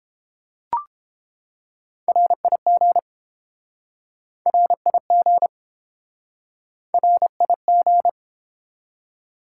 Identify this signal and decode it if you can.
Morse code, heard as a keyed tone, about 30 wpm, 700 Hz. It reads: RIG RIG RIG